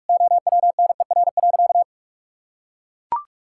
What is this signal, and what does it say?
Morse code, heard as a keyed tone, about 42 wpm, 700 Hz. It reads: OWNER.
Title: Sentences from Top 1600 Words 45wpm